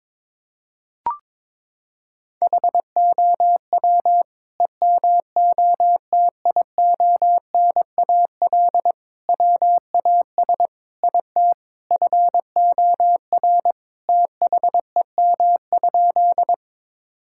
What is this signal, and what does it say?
Morse code, heard as a keyed tone, about 22 wpm, 700 Hz. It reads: HOW EMOTIONAL WAS IT FOR THEM?
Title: Sentences from Top 1000 Words - Review Part 5 - 22wpm